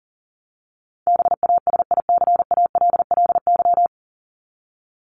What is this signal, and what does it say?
Morse code, heard as a keyed tone, about 40 wpm, 700 Hz. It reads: BASICALLY
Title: Top 1251-1500 Words 40wpm